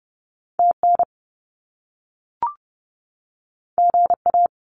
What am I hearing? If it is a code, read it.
Morse code, heard as a keyed tone, about 30 wpm, 700 Hz. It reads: TN GA
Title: US States 30wpm